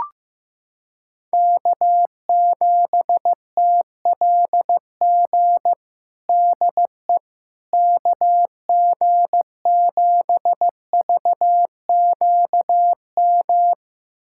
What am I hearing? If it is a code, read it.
Morse code, heard as a keyed tone, about 15 wpm, 700 Hz. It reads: K7TLG DE KG7VQM
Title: USA Callsign DE Callsign 15wpm